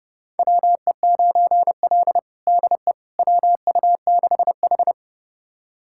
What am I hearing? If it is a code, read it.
Morse code, heard as a keyed tone, about 32 wpm, 700 Hz. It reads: WE9L DE WU6H
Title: USA Callsign DE Callsign - 4 Characters - 30wpm